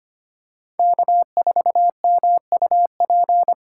Morse code, heard as a keyed tone, about 25 wpm, 700 Hz. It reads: K4MUP
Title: USA Call Signs - 6x Character Spacing - 25wpm